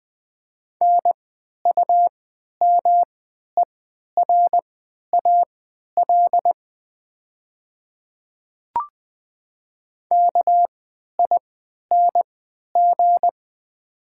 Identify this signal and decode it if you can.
Morse code, heard as a keyed tone, about 20 wpm, spaced as for 11 wpm, 700 Hz. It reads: NUMERAL KING
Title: 3x Character Spacing - Top 500 Words - 20wpm